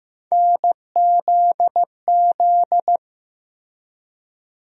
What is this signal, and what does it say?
Morse code, heard as a keyed tone, about 15 wpm, 700 Hz. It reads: NZZ
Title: USA Call Sign Postfixes - 3 Letters 15wpm